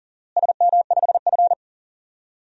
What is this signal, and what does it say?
Morse code, heard as a keyed tone, about 37 wpm, 700 Hz. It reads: SM5F
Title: DXCC-100 Call Signs - 4 Characters - Repeated 3x - 40wpm